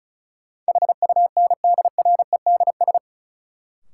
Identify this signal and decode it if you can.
Morse code, heard as a keyed tone, about 36 wpm, 700 Hz. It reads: HUNDREDS
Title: Sets of 4 Words - New Word Drill - 35wpm